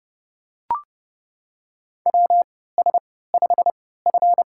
Morse code, heard as a keyed tone, about 30 wpm, spaced as for 17 wpm, 700 Hz. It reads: WS5F
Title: USA Call Signs - 3x Character Spacing - 30wpm